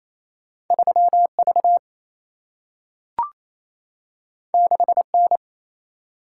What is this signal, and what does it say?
Morse code, heard as a keyed tone, about 28 wpm, 700 Hz. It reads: 3 V 6 N
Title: Single-Letter Single-Number 28wpm